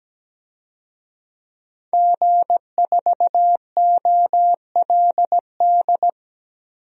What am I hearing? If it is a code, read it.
Morse code, heard as a keyed tone, about 17 wpm, 700 Hz. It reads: G4OLD
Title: DXCC-100 Call Signs - 5 Characters - 17wpm